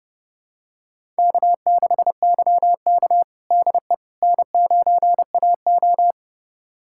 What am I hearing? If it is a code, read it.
Morse code, heard as a keyed tone, about 32 wpm, 700 Hz. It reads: K6YK DE N9AO
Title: USA Callsign DE Callsign - 4 Characters - 30wpm